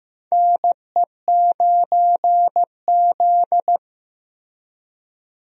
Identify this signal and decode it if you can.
Morse code, heard as a keyed tone, about 15 wpm, 700 Hz. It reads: NE9Z
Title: USA Call Signs - 4 Characters - Repeated 2x - 15wpm